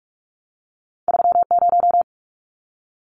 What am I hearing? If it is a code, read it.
Morse code, heard as a keyed tone, about 46 wpm, 700 Hz. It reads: 30
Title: Numbers - 2-Digit 45wpm